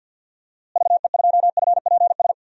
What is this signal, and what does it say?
Morse code, heard as a keyed tone, about 51 wpm, 700 Hz. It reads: VE2FPS
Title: DXCC-100 Call Signs - 6 Characters - Repeated 3x - 50wpm